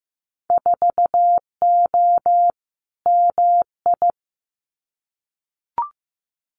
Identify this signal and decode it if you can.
Morse code, heard as a keyed tone, about 16 wpm, 700 Hz. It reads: 4O MI
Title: Winter Field Day Exchanges Encore 15wpm